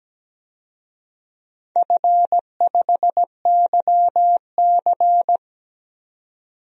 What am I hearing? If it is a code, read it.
Morse code, heard as a keyed tone, about 17 wpm, 700 Hz. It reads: F5YC
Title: DXCC-100 Call Signs - 4 Characters - 17wpm